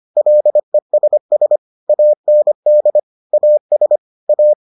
Morse code, heard as a keyed tone, about 25 wpm, 600 Hz. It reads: LESS AND AS A